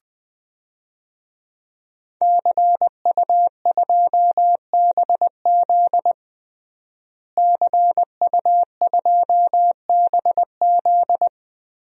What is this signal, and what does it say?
Morse code, heard as a keyed tone, about 20 wpm, 700 Hz. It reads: CU2BZ CU2BZ
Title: DXCC-100 Call Signs - 5 Characters - Repeated 2x - 20wpm